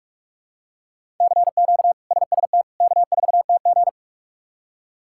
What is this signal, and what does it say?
Morse code, heard as a keyed tone, about 46 wpm, 700 Hz. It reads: CQ SST K4TG
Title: K1USN SST QSO Practice - 45wpm